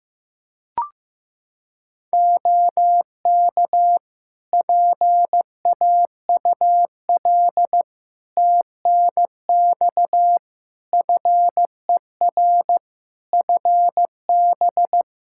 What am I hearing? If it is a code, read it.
Morse code, heard as a keyed tone, about 15 wpm, 700 Hz. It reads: OK PAUL TNX FER FB
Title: K1USN SST QSO Practice 15wpm